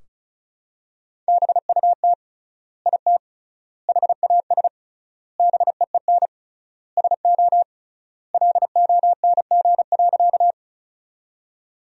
Morse code, heard as a keyed tone, about 36 wpm, 700 Hz. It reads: BUT IT HAS BEEN SO LONG.
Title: 3x Word-Spacing - Sentences from Top 100 Words - 35wpm